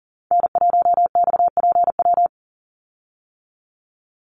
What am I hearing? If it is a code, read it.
Morse code, heard as a keyed tone, about 38 wpm, 700 Hz. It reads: N1XPW
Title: USA Call Signs 40wpm